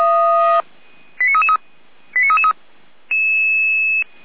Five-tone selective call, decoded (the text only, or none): 89333 89333